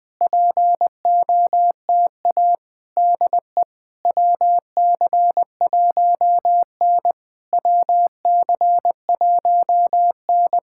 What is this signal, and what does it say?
Morse code, heard as a keyed tone, about 21 wpm, 700 Hz. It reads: POTA DE WC1N WC1N
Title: POTA QSO Practice - 20wpm